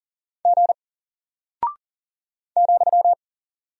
Morse code, heard as a keyed tone, about 39 wpm, 700 Hz. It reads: G ,
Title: Single Character (Letter:Number:Punctuation:Prosign) Alt 40wpm